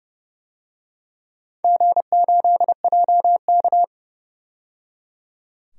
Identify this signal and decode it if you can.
Morse code, heard as a keyed tone, about 30 wpm, 700 Hz. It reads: G8JK
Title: DXCC-100 Call Signs - 4 Characters - 30wpm